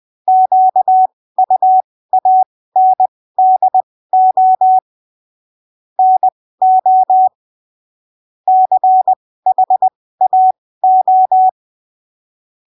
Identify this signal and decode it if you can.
Morse code, heard as a keyed tone, about 20 wpm, spaced as for 15 wpm, 750 Hz. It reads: QUANDO NO CHAO